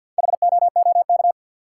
Morse code, heard as a keyed tone, about 52 wpm, 700 Hz. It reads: HOOK
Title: Four Letter Words - 50wpm